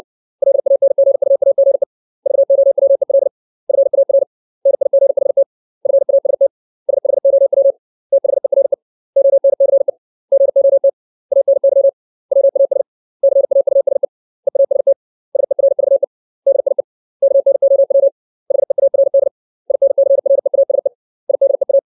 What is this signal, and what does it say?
Morse code, heard as a keyed tone, about 60 wpm, 550 Hz. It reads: LANGUAGE VOWEL FAR NIGHT FAST SHOW THERE ONCE GOT ANY WAS CAUSE EAST SERVE DIE KNOW HEARD INCREASE IDEA
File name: 632 words / 632 Words 60WPM